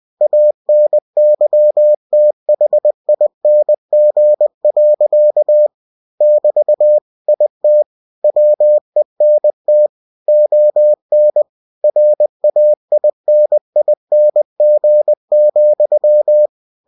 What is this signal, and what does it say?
Morse code, heard as a keyed tone, about 20 wpm, 600 Hz. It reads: ANYTHING. <BT> IT WENT ON RAINING,
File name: cw-books / winnie-the-pooh / winnie-20wpm-page-0091